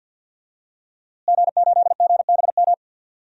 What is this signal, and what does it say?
Morse code, heard as a keyed tone, about 48 wpm, 700 Hz. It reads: G8GBM
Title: DXCC-100 Call Signs - 5 Characters - Repeated 2x - 50wpm